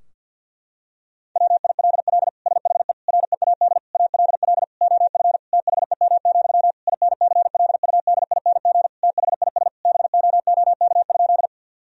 Morse code, heard as a keyed tone, about 50 wpm, 700 Hz. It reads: WILL SHE READ ALL OF THEM, INCLUDING THIS BOOK?
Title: Sentences from Top 1000 Words - Review Part 4 - 50wpm